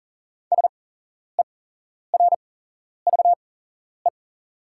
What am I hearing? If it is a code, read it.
Morse code, heard as a keyed tone, about 40 wpm, spaced as for 11 wpm, 690 Hz. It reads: SERVE
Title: 8x Character Spacing - Top 500 Words - 40wpm